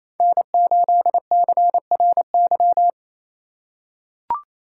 Morse code, heard as a keyed tone, about 28 wpm, 700 Hz. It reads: N8CRY
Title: USA Call Signs - 5 Characters - 2x Character Spacing - 28wpm